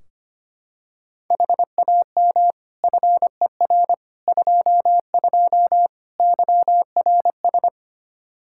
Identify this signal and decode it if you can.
Morse code, heard as a keyed tone, about 25 wpm, 700 Hz. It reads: HAM FER 22 YRS
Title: Practice QSO - Exchange 3 - More Info - 25wpm